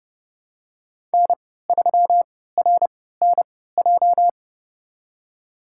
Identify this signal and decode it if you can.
Morse code, heard as a keyed tone, about 30 wpm, spaced as for 17 wpm, 700 Hz. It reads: N3RNJ
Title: USA Call Signs - 5 Characters - 3x Character Spacing - 30wpm